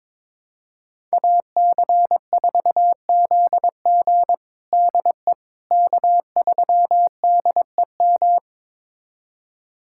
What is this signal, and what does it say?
Morse code, heard as a keyed tone, about 22 wpm, 700 Hz. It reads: AC4ZG DE K3DEM
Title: USA Callsign DE Callsign - 5 Characters - 22wpm